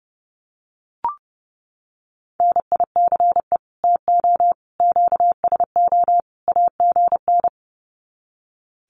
Morse code, heard as a keyed tone, about 30 wpm, 700 Hz. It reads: NICE TO QSO AGN
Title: Common QSO Phrases 30wpm